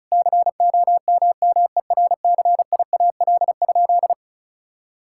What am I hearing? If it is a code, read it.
Morse code, heard as a keyed tone, about 34 wpm, 700 Hz. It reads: COMMERCIAL?
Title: Sentences from Top 1000 Words - Review Part 5 - 35wpm